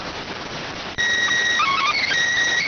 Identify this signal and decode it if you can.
SSTV (slow-scan television), PD 290.